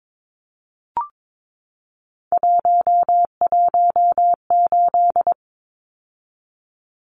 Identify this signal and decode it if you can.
Morse code, heard as a keyed tone, about 22 wpm, 700 Hz. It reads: 118
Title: Numbers - 3-Digit 22wpm